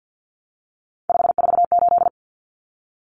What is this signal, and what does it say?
Morse code, heard as a keyed tone, about 50 wpm, 700 Hz. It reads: 548
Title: Numbers - 3-Digit 50wpm